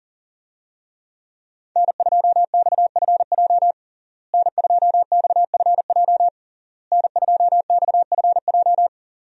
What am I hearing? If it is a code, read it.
Morse code, heard as a keyed tone, about 40 wpm, 700 Hz. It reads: N2XFJ N2XFJ N2XFJ
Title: USA Call Signs - 5 Characters - Repeated 3x - 40wpm